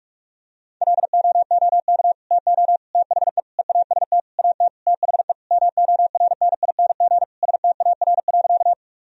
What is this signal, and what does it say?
Morse code, heard as a keyed tone, about 46 wpm, 700 Hz. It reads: LOOK TO THE EAST AT THE MORNING STAR.
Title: Sentences from Top 1000 Words - Review Part 3 - 45wpm